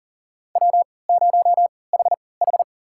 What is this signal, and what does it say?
Morse code, heard as a keyed tone, about 40 wpm, spaced as for 23 wpm, 700 Hz. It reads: W0HH
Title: USA Call Signs - 4 Characters - 3x Character Spacing - 40wpm